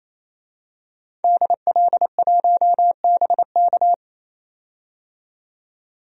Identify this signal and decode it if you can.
Morse code, heard as a keyed tone, about 28 wpm, 700 Hz. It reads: DL1BK
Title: DXCC-100 Call Signs - 5 Characters - 28wpm